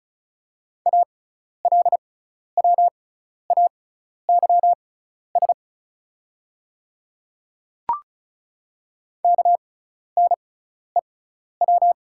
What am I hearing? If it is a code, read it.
Morse code, heard as a keyed tone, about 34 wpm, spaced as for 12 wpm, 690 Hz. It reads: ALWAYS KNEW
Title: 6x Character Spacing - Top 500 Words - 35wpm